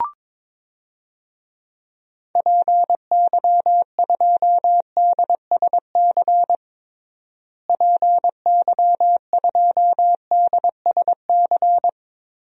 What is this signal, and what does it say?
Morse code, heard as a keyed tone, about 22 wpm, 700 Hz. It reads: PY2DSC PY2DSC
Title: DXCC-100 Call Signs - 6 Characters - Repeated 2x - 22wpm